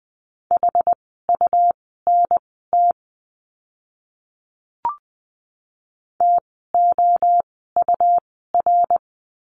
Morse code, heard as a keyed tone, about 20 wpm, spaced as for 15 wpm, 700 Hz. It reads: HUNT TOUR
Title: 101 Lesson 17f - Letter u - Words 20wpm